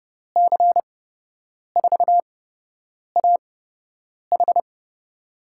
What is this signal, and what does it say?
Morse code, heard as a keyed tone, about 30 wpm, spaced as for 8 wpm, 700 Hz. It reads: C4AH
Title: USA Call Signs - 6 Characters - 8x Character Spacing - 30wpm